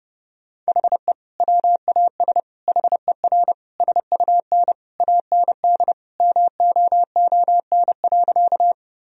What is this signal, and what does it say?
Morse code, heard as a keyed tone, about 32 wpm, 700 Hz. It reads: HE WAS HER SUN AND MOON.
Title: Sentences from Top 1000 Words - Review Part 3 - 30wpm